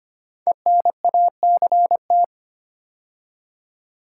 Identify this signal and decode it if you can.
Morse code, heard as a keyed tone, about 25 wpm, 700 Hz. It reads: ENACT
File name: 5 Letter Words - First 3 Letters Given - 25wpm